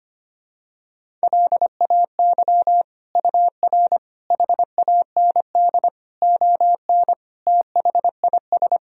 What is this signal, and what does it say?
Morse code, heard as a keyed tone, about 25 wpm, 700 Hz. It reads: LAY UR HAND ON THIS
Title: Sentences from Top 400 Words and Abbreviations 25wpm